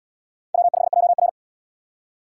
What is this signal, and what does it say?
Morse code, heard as a keyed tone, about 68 wpm, 700 Hz. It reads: W5JR